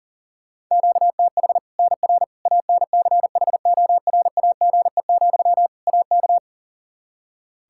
Morse code, heard as a keyed tone, about 40 wpm, 700 Hz. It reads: QTH NR ANCHORAGE, AK
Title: K1USN SST QSO Practice 40wpm